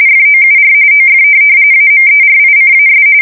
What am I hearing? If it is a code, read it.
RTTY, about 45.45 baud. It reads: FOUR OF AMERICAS
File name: fsk170 / USS-New-Jersey